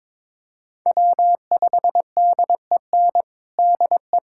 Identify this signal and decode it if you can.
Morse code, heard as a keyed tone, about 21 wpm, 700 Hz. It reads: W5DEN DE
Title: USA Callsign DE Callsign 22wpm